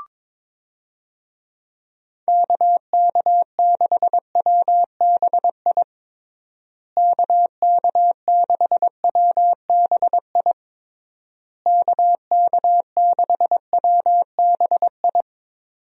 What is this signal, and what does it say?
Morse code, heard as a keyed tone, about 21 wpm, 700 Hz. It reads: KK6WBI KK6WBI KK6WBI
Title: USA Call Signs - 6 Characters - Repeated 3x - 22wpm